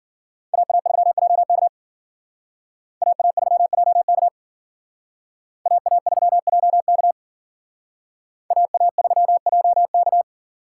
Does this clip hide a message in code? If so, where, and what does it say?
Yes: Morse code from 0.5 to 10.2 s, heard as a keyed tone, about 51 wpm, 700 Hz. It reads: AA3JK AA3JK AA3JK AA3JK